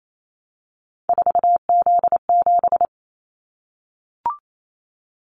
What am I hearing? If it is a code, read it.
Morse code, heard as a keyed tone, about 28 wpm, 700 Hz. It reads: 4Z7
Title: Callsign Prefixes for Top 50 DXCC - 28wpm